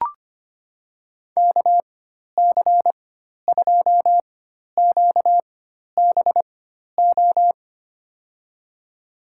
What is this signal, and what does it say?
Morse code, heard as a keyed tone, about 25 wpm, spaced as for 12 wpm, 700 Hz. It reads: KC2QBO
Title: USA Call Signs - 6 Characters - 4x Character Spacing - 25wpm